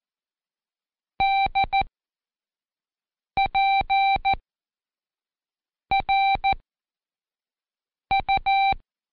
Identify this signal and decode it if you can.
Morse code, heard as a keyed tone, about 14 wpm, spaced as for 5 wpm, 780 Hz. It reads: DPRU
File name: Code Quick / AudioDownld1 / Seg26-Day3-Review UDPR